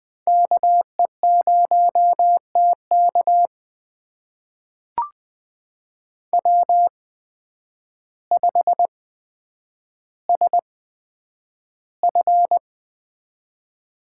Morse code, heard as a keyed tone, about 20 wpm, spaced as for 9 wpm, 700 Hz. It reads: KE0TK W5SF
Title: USA Call Signs - 5 Characters - 8x Character Spacing - 20wpm